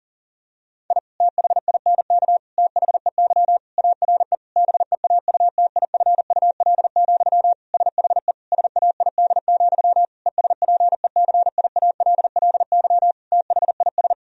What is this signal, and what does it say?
Morse code, heard as a keyed tone, about 40 wpm, 700 Hz. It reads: I THINK THEY ARE BEAUTIFUL, SHE SAID, ESPECIALLY THIS
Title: Sentences from Top 1000 Words - Review Part 4 - 40wpm